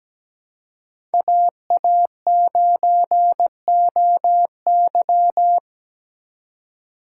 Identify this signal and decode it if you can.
Morse code, heard as a keyed tone, about 20 wpm, 700 Hz. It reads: AA9OY
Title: Speed Racing - USA 2x2 Call Signs 17wpm